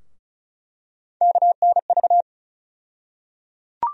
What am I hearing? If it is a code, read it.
Morse code, heard as a keyed tone, about 34 wpm, 700 Hz. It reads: KNV
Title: USA Call Sign Postfixes - 3 Letters 35wpm